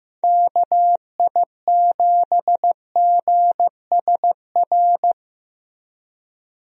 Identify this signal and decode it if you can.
Morse code, heard as a keyed tone, about 15 wpm, 700 Hz. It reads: KI7GSR
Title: USA Call Signs - 6 Characters - Repeated 3x - 15wpm